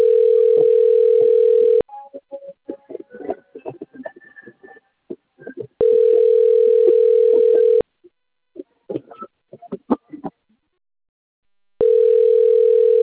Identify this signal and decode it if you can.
Ringback tone (North America), 440 + 480 Hz.